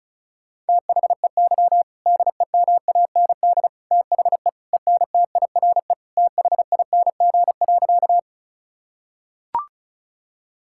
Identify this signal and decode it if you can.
Morse code, heard as a keyed tone, about 34 wpm, 700 Hz. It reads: THEY DEMAND THE ENTIRE THING.
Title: Sentences from Top 1000 Words - Review Part 5 - 35wpm